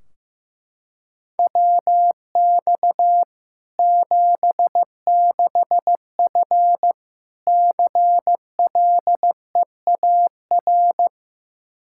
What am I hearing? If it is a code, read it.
Morse code, heard as a keyed tone, about 15 wpm, 700 Hz. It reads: WX 76F CLEAR
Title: K1USN SST QSO Practice 15wpm